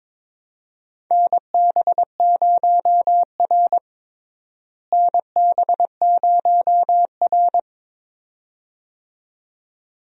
Morse code, heard as a keyed tone, about 22 wpm, 700 Hz. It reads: NB0R NB0R
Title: USA Call Signs - 4 Characters - Repeated 2x - 22wpm